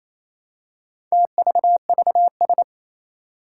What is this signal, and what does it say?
Morse code, heard as a keyed tone, about 28 wpm, 700 Hz. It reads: TVVS
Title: Four Challenging Letter-Number Combinations 28wpm